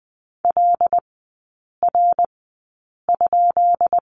Morse code, heard as a keyed tone, about 19 wpm, spaced as for 8 wpm, 700 Hz. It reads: LR?
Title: 159 Lesson 26e - Letter g - Three Characters 20wpm